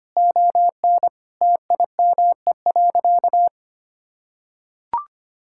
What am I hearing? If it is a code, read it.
Morse code, heard as a keyed tone, about 25 wpm, 700 Hz. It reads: ON TIME.
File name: practice / 25wpm / Sentences with 2-Word Phrases - 25wpm